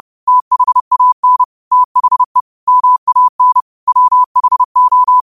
Morse code, heard as a keyed tone, about 30 wpm, 1000 Hz. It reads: THAN THE MAN WHO
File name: qotdfiles / 30wpm